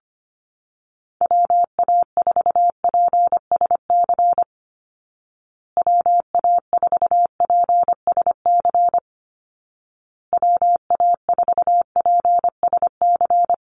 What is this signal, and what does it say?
Morse code, heard as a keyed tone, about 25 wpm, 700 Hz. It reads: WA4PSC WA4PSC WA4PSC
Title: Call Signs Repeated 2x - 25wpm